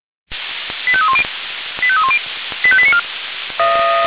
Five-tone selective call (ZVEI), heard as quick runs of five tones, tones of about 70 ms. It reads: 85310 85310 85894
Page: SDR/ZVEI